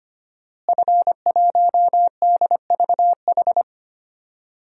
Morse code, heard as a keyed tone, about 25 wpm, 700 Hz. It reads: F1DVH
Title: DXCC-100 Call Signs - 5 Characters - Repeated 2x - 25wpm